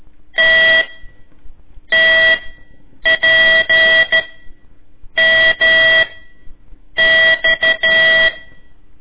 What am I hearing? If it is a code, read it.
Morse code, heard as a keyed tone, about 13 wpm, spaced as for 7 wpm, 1950 Hz. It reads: TTPMX
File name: ik1weg2 / CORSI CW / lez4-1